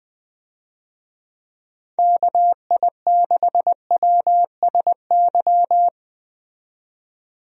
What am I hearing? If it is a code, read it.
Morse code, heard as a keyed tone, about 20 wpm, 700 Hz. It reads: KI6WSY
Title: USA Call Signs - 6 Characters - 20wpm